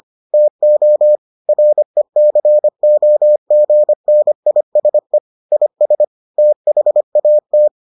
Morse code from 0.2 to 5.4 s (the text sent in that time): TO RECOGNISE